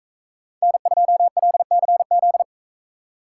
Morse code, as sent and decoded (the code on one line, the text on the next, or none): -. ..--- .-.. -.-. --..
N2LCZ